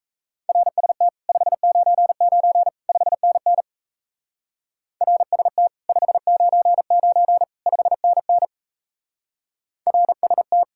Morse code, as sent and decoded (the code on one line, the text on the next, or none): .-. ... - / ..... ----. ----. / ..... -. -. / .-. ... - / ..... ----. ----. / ..... -. -. / .-. ... -
RST 599 5NN RST 599 5NN RST